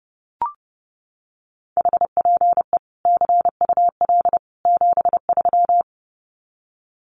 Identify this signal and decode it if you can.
Morse code, heard as a keyed tone, about 30 wpm, 700 Hz. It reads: HPE CUL 73
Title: Common QSO Phrases 30wpm